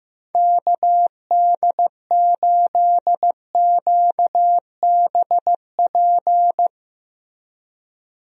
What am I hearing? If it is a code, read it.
Morse code, heard as a keyed tone, about 15 wpm, 700 Hz. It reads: KD8QBP
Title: USA Call Signs - 6 Characters - Repeated 3x - 15wpm